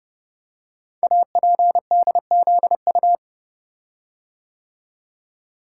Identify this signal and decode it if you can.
Morse code, heard as a keyed tone, about 30 wpm, 690 Hz. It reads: APDZU
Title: Code Groups - 5 Letters - 30wpm